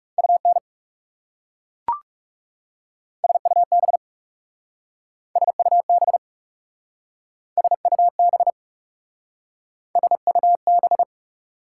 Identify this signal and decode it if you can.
Morse code, heard as a keyed tone, about 40 wpm, 690 Hz. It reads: UN SUB SUB SUB SUB